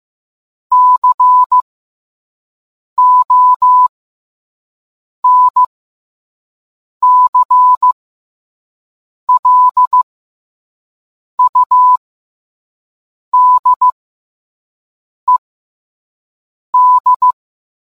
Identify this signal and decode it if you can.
Morse code, heard as a keyed tone, about 15 wpm, spaced as for 5 wpm, 1000 Hz. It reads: CONCLUDED